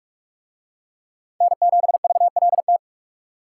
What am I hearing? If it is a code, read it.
Morse code, heard as a keyed tone, about 43 wpm, 700 Hz. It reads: N7VLT